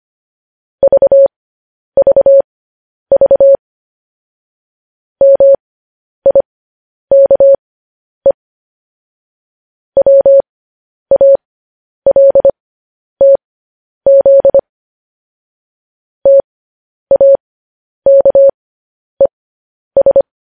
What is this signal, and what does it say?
Morse code, heard as a keyed tone, about 25 wpm, spaced as for 10 wpm, 550 Hz. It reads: VVV MIKE WALTZ TAKES
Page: news-headlines-in-morse-code-at-10-wpm